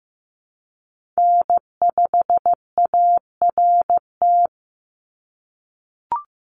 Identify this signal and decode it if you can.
Morse code, heard as a keyed tone, about 15 wpm, 700 Hz. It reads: N5ART
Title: USA Call Signs 15wpm